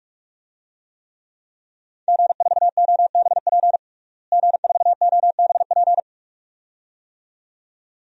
Morse code, heard as a keyed tone, about 46 wpm, 700 Hz. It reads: G4OBP G4OBP
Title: DXCC-100 Call Signs - 5 Characters - Repeated 2x - 45wpm